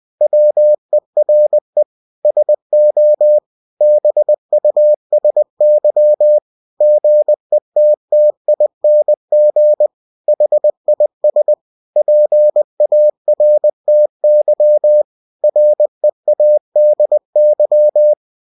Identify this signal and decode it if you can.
Morse code, heard as a keyed tone, about 20 wpm, 600 Hz. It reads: WERE SO BUSY GETTING HIS PARTY READY